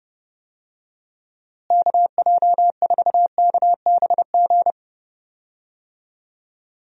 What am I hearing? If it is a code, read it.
Morse code, heard as a keyed tone, about 31 wpm, 700 Hz. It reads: KJ4KBG
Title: USA Call Signs - 6 Characters - Repeated 2x - 30wpm